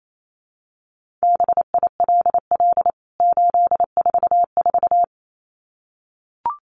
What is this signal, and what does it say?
Morse code, heard as a keyed tone, about 28 wpm, 700 Hz. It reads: BILL 844
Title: CWT Exchange 28wpm